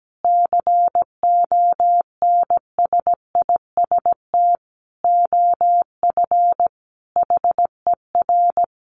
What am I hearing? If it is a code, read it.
Morse code, heard as a keyed tone, about 17 wpm, 700 Hz. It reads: CONSIST OF HER
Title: Sentences from Top 1400 Words 17wpm